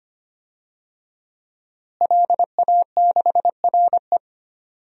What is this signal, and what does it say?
Morse code, heard as a keyed tone, about 25 wpm, 700 Hz. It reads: LA6RE
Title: DXCC-100 Call Signs - 5 Characters - Repeated 3x - 25wpm